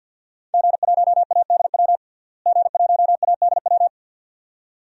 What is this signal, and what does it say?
Morse code, heard as a keyed tone, about 52 wpm, 700 Hz. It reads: G1ADW G1ADW
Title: DXCC-100 Call Signs - 5 Characters - Repeated 2x - 50wpm